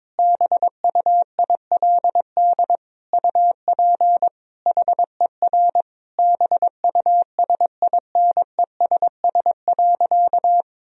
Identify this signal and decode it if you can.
Morse code, heard as a keyed tone, about 22 wpm, 700 Hz. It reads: BUILD UP HER BUSINESS.
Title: Sentences from Top 1000 Words - Review Part 1 - 22wpm